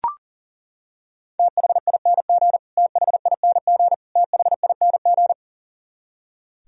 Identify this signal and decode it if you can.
Morse code, heard as a keyed tone, about 40 wpm, 700 Hz. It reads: THING THING THING
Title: Top 100 Words Repeated 40wpm